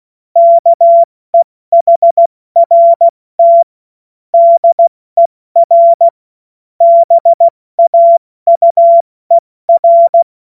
Morse code, heard as a keyed tone, about 16 wpm, 680 Hz. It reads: KEHRT DER BAUER
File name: Witz-16-September-2025-70-F80